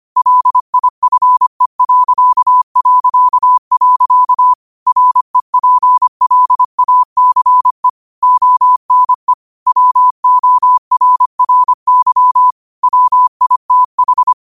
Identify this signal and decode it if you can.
Morse code, heard as a keyed tone, about 25 wpm, 1000 Hz. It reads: LIFE... REPLACE ONE WORRY WITH
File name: qotdfiles / 25wpm